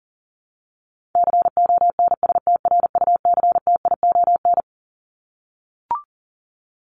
Morse code, heard as a keyed tone, about 40 wpm, 700 Hz. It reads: CONSTRUCTION
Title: Top 1501-1750 Words 40wpm